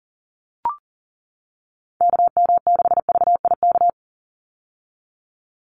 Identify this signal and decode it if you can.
Morse code, heard as a keyed tone, about 40 wpm, 700 Hz. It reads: KM6VIK
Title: USA Call Signs 40wpm